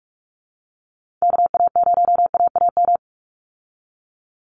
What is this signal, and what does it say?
Morse code, heard as a keyed tone, about 44 wpm, 700 Hz. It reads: KA0AAM
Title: Call Signs Repeated 3x - 45wpm